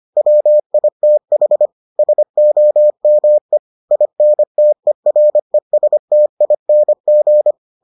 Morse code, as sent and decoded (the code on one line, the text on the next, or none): .-- .. - .... / ... --- -- . / .. -. - . .-. ... - .. -. --.
WITH SOME INTERESTING